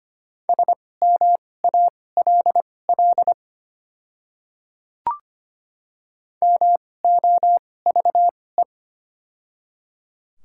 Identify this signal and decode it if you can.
Morse code, heard as a keyed tone, about 25 wpm, spaced as for 18 wpm, 690 Hz. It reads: SMALL MOVE
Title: 2x Character Spacing - Top 300 Words - 25wpm